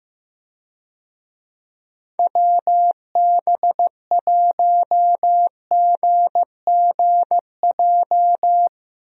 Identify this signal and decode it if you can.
Morse code, heard as a keyed tone, about 15 wpm, 700 Hz. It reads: WB1GGJ
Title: USA Call Signs - 6 Characters - Repeated 2x - 15wpm